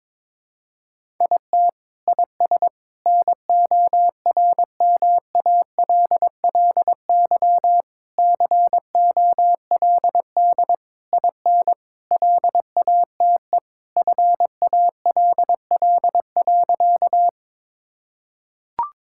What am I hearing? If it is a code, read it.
Morse code, heard as a keyed tone, about 22 wpm, 700 Hz. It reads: IT IS NORMALLY COLD IN LATE FALL.
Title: Sentences from Top 1000 Words - Review Part 6 - 22wpm